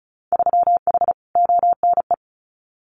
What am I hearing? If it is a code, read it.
Morse code, heard as a keyed tone, about 36 wpm, 700 Hz. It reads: 3H ONE
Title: Winter Field Day Exchanges Encore 35wpm